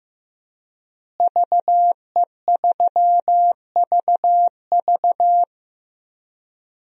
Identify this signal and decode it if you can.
Morse code, heard as a keyed tone, about 15 wpm, 700 Hz. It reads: VE3VV
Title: DXCC-100 Call Signs - 5 Characters - Repeated 3x - 15wpm